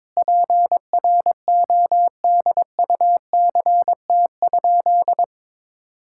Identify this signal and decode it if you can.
Morse code, heard as a keyed tone, about 21 wpm, 700 Hz. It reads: PRODUCT?
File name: Sentences with 9-Word Phrases - 22wpm